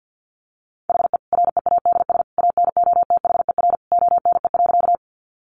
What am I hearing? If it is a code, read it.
Morse code, heard as a keyed tone, about 50 wpm, 700 Hz. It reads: HE READS ANOTHER ONE.
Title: Additional Easy Phrases 50wpm